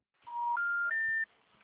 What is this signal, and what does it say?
Special information tone.